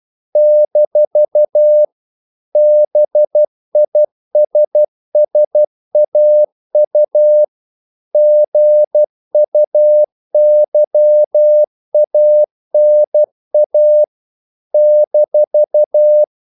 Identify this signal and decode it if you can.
Morse code, heard as a keyed tone, about 12 wpm, 600 Hz. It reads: - BISSAU GUYANA -
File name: CW / Practice / NationalCapitals12